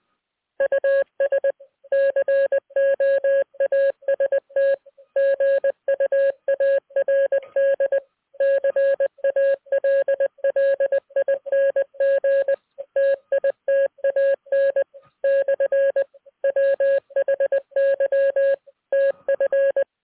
Morse code, heard as a keyed tone, about 19 wpm, 550 Hz. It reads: US COAST GUARD CALLING TITAN / WHY TF